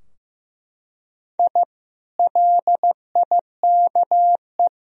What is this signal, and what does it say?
Morse code, heard as a keyed tone, about 15 wpm, 700 Hz. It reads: I LIKE
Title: Sentences from Top 1000 Words - Review Part 3 - 15wpm